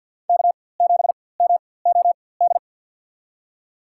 Morse code, heard as a keyed tone, about 51 wpm, spaced as for 23 wpm, 700 Hz. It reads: K7MOD
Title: USA Call Signs - 4x Character Spacing - 50wpm